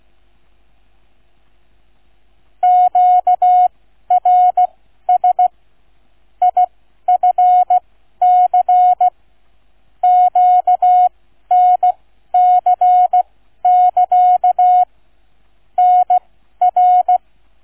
Morse code, heard as a keyed tone, about 16 wpm, spaced as for 12 wpm, 730 Hz. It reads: QRS IFC QNC<KA> NR
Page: QRS International Flight Contest 2025 Demo Flight NR 00 Sound files Listen-to or download incoming QNC sound file